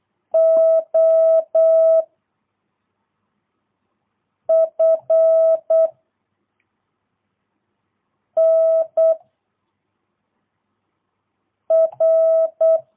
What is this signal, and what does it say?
Morse code, heard as a keyed tone, about 8 wpm, 660 Hz. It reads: O F N R
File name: jh8cbh / 2022cw